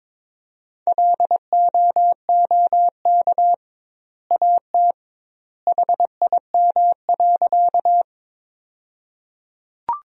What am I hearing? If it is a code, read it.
Morse code, heard as a keyed tone, about 22 wpm, 700 Hz. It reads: LOOK AT HIM.
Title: 2x Word-Spacing - Sentences from Top 100 Words - 22wpm